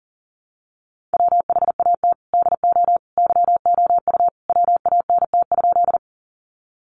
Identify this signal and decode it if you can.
Morse code, heard as a keyed tone, about 40 wpm, 700 Hz. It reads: WHAT DO YOU WANT?
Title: Sentences from Top 200 Words 40wpm